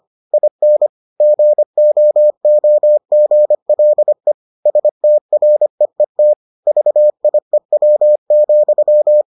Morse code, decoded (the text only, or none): IN GOOGLE STREET VIEW,